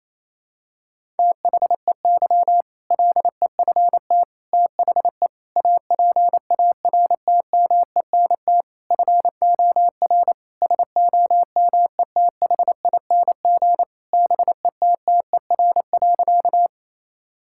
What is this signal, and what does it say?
Morse code, heard as a keyed tone, about 28 wpm, 700 Hz. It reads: THEY LEFT THE APARTMENT FOR SOMETHING BETTER.
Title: Sentences from Top 1000 Words - Review Part 6 - 28wpm